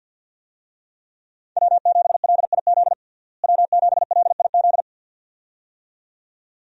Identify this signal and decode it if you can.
Morse code, heard as a keyed tone, about 52 wpm, 700 Hz. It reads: W7LIZ W7LIZ
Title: USA Call Signs - 5 Characters - Repeated 2x - 50wpm